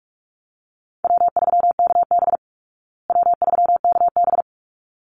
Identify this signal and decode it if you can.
Morse code, heard as a keyed tone, about 44 wpm, 700 Hz. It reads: W3KB W3KB
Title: Easy Call Signs Repeated 2x - 45wpm